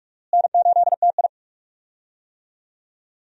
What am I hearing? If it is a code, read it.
Morse code, heard as a keyed tone, about 42 wpm, 700 Hz. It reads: N8TI
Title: USA Call Signs - 4 Characters - Repeated 2x - 45wpm